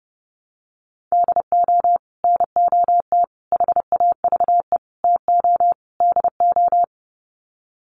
Morse code, heard as a keyed tone, about 28 wpm, 700 Hz. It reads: DO NOT HAVE TO DO
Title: Sets of 5 Words 30wpm